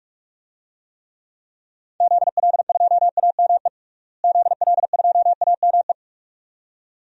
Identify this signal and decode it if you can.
Morse code, heard as a keyed tone, about 46 wpm, 700 Hz. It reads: ZL2AME ZL2AME
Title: DXCC-100 Call Signs - 6 Characters - Repeated 2x - 45wpm